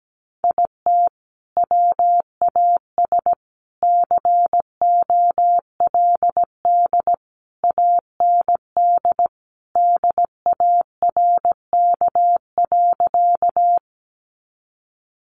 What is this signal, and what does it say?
Morse code, heard as a keyed tone, about 17 wpm, 700 Hz. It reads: IT WAS COLD AND DARK.
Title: Sentences from Top 500 Words 17wpm